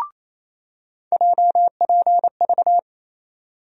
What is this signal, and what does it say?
Morse code, heard as a keyed tone, about 28 wpm, 700 Hz. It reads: JPV
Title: USA Call Sign Postfixes - 3 Letters 28wpm